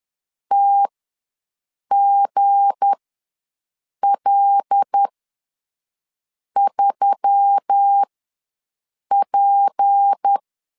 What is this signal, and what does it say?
Morse code, heard as a keyed tone, about 10 wpm, spaced as for 5 wpm, 800 Hz. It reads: TGL3P
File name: LEZIONI / 7 GF9Z / 7 G ws 10wpm 3843